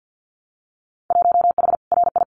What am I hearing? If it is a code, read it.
Morse code, heard as a keyed tone, about 50 wpm, 700 Hz. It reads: 1H RI
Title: Winter Field Day Exchanges 50wpm